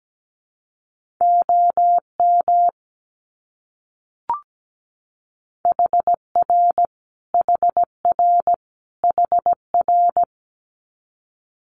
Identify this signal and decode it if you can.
Morse code, heard as a keyed tone, about 17 wpm, 700 Hz. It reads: OM HR HR HR
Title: Abbreviations Repeated 3x - 17wpm